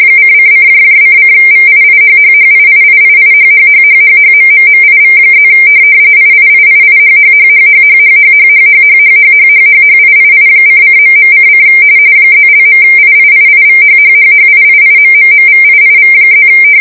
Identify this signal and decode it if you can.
RTTY, about 45.45 baud. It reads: :::::VIIHHHIIIIIHHHHHHMMMIV+ IIIIIHHHHHHHHHHHHHHHHHHIII::..:: +).IIIIVHHHHIIIMMMVIIV+